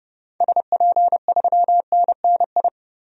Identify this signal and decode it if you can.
Morse code, heard as a keyed tone, about 30 wpm, 700 Hz. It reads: SP3NNI
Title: DXCC-100 Call Signs - 6 Characters - 30wpm